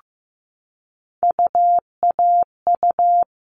Morse code, heard as a keyed tone, about 15 wpm, 700 Hz. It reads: UAU
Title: All Cut Numbers - 3-Digit 15wpm